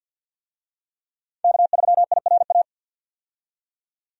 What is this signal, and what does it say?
Morse code, heard as a keyed tone, about 50 wpm, 700 Hz. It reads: K3IRA
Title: USA Call Signs - 5 Characters - 50wpm